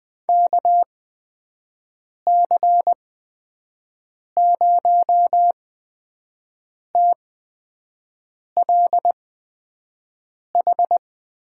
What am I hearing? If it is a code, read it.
Morse code, heard as a keyed tone, about 20 wpm, spaced as for 5 wpm, 700 Hz. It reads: KC0TLH
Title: USA Call Signs - 6 Characters - 8x Character Spacing - 20wpm